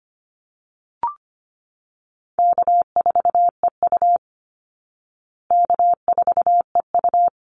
Morse code, heard as a keyed tone, about 25 wpm, 700 Hz. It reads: K4EU K4EU
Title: Easy Call Signs Repeated 3x - 25wpm